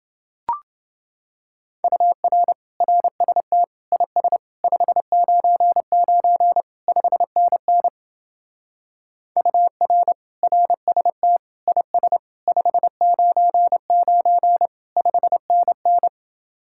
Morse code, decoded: UR RST IS 599 5NN UR RST IS 599 5NN

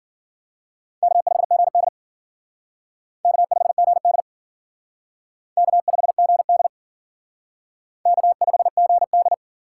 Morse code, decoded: K5GD K5GD K5GD K5GD

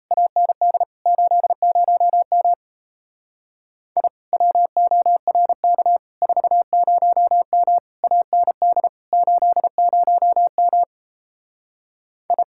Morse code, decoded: AND 80M I WORK 40M AND 80M I

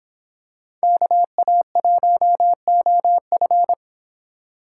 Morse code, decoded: KA1OF

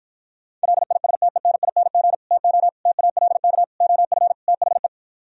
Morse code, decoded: LISTENING TO TALK OF THE

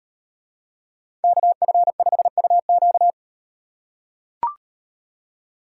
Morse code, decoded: KF5UQ

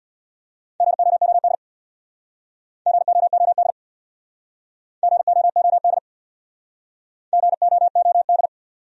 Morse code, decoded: GOOD GOOD GOOD GOOD

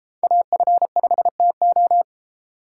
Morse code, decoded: AF5TO

